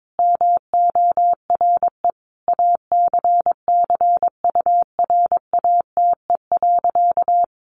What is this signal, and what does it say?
Morse code, heard as a keyed tone, about 22 wpm, 700 Hz. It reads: MORE ACCURATE.
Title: Sentences from Top 1200 Words 22wpm